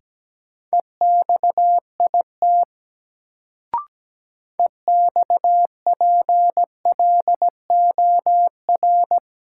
Morse code, heard as a keyed tone, about 17 wpm, 700 Hz. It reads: EXIT EXPLOR
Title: Sentences from Top 1100 Words - New Word Drill - 17wpm